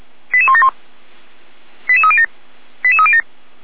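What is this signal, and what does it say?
Five-tone selective call (ZVEI), heard as quick runs of five tones, tones of about 70 ms. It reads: 88171 88307 88307
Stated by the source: ZVEI five-tone call received by SDR, recorded by the Orlamünde fire brigade (feuerwehr-orlamuende).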